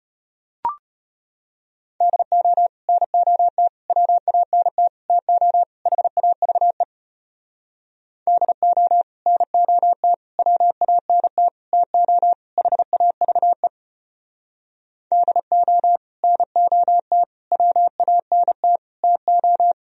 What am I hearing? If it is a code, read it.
Morse code, heard as a keyed tone, about 33 wpm, 700 Hz. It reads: DO NOT WANT TO HAVE DO NOT WANT TO HAVE DO NOT WANT TO